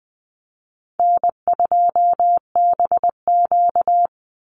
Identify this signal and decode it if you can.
Morse code, heard as a keyed tone, about 20 wpm, 700 Hz. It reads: N2BQ
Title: USA Call Signs 20wpm